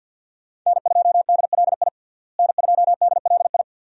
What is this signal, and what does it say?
Morse code, heard as a keyed tone, about 53 wpm, 700 Hz. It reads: N2DLI N2DLI
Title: USA Call Signs - 5 Characters - Repeated 2x - 50wpm